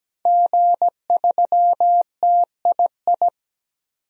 Morse code, heard as a keyed tone, about 17 wpm, 700 Hz. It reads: G3TII
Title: DXCC-100 Call Signs - 5 Characters - 17wpm